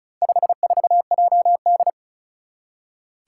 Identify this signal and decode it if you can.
Morse code, heard as a keyed tone, about 36 wpm, 700 Hz. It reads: 54JD
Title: Four Challenging Letter-Number Combinations 35wpm